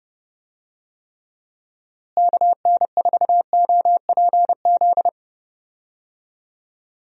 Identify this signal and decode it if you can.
Morse code, heard as a keyed tone, about 30 wpm, 700 Hz. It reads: KN4OPZ
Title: USA Call Signs - 6 Characters - Repeated 2x - 30wpm